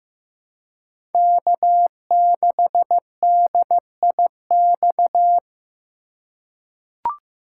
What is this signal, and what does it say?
Morse code, heard as a keyed tone, about 15 wpm, 700 Hz. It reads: K6DIX
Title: USA Call Signs - 5 Characters - 4x Character Spacing - 15wpm